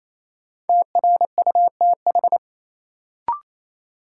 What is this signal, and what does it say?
Morse code, heard as a keyed tone, about 28 wpm, 700 Hz. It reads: TRUTH
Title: Sentences from Top 800 Words - New Word Drill - 28wpm